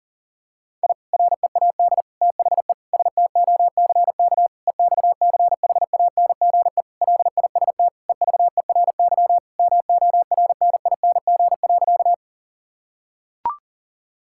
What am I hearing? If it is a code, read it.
Morse code, heard as a keyed tone, about 37 wpm, 700 Hz. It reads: I READ THE STOCK EXCHANGE LIST EVERY MORNING.
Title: Sentences from Top 1000 Words - Review Part 6 - 40wpm